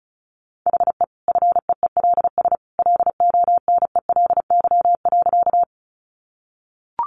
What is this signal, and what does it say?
Morse code, heard as a keyed tone, about 34 wpm, 700 Hz. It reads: HE FEELS LONELY.
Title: Easy Phrases 35wpm